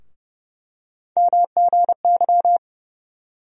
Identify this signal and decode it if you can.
Morse code, heard as a keyed tone, about 30 wpm, 700 Hz. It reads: MGY
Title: USA Call Sign Postfixes - 3 Letters 30wpm